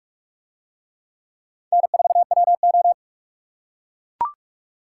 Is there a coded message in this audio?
Yes: Morse code from 1.7 to 2.9 s, heard as a keyed tone, about 44 wpm, 700 Hz. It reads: N4WO